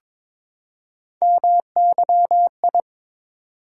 Morse code, heard as a keyed tone, about 22 wpm, 700 Hz. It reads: MYI
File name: practice / 22wpm / Three Letter-Number Combinations - 22wpm